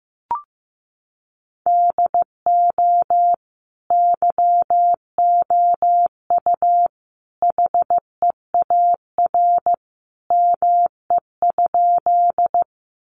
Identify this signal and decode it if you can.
Morse code, heard as a keyed tone, about 15 wpm, 700 Hz. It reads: DO YOU HEAR ME?
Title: Sentences from Top 300 Words 15wpm